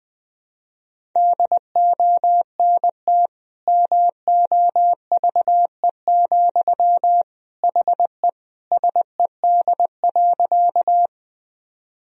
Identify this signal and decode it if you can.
Morse code, heard as a keyed tone, about 20 wpm, 700 Hz. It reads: DONT MOVE, HE SED.
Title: Sentences from Top 300 Words and Abbreviations 20wpm